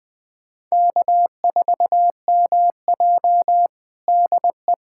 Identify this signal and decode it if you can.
Morse code, heard as a keyed tone, about 20 wpm, 700 Hz. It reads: K4MJ DE
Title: USA Callsign DE Callsign - 4 Characters - 20wpm